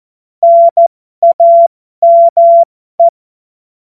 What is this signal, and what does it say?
Morse code, heard as a keyed tone, about 14 wpm, spaced as for 12 wpm, 680 Hz. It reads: NAME
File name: Morsekurs / Witze